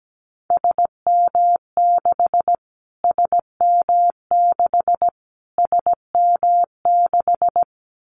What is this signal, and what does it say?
Morse code, heard as a keyed tone, about 17 wpm, 700 Hz. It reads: SM6 SM6 SM6
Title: Callsign Prefixes for Top 25 DXCC - Repeated 3x - 17wpm